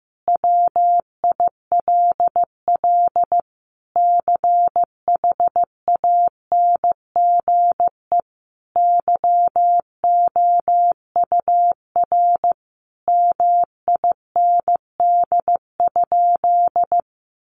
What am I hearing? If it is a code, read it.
Morse code, heard as a keyed tone, about 14 wpm, 700 Hz. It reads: WILL CHANGE YOUR MIND?
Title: Sentences from Top 1300 Words 15wpm